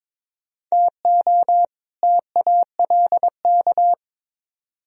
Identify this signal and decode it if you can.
Morse code, heard as a keyed tone, about 26 wpm, 700 Hz. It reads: TO TALK
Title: Speed-Racing - Sets of 2 Words 22wpm